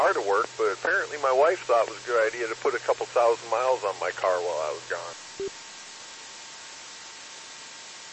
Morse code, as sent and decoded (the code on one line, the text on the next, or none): none